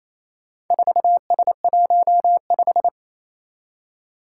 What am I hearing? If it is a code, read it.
Morse code, heard as a keyed tone, about 29 wpm, 700 Hz. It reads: 4S15
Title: Four Challenging Letter-Number Combinations 28wpm